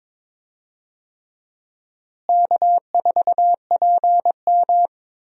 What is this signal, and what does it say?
Morse code, heard as a keyed tone, about 22 wpm, 700 Hz. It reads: K4PM
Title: USA Call Signs - 4 Characters - Repeated 3x - 22wpm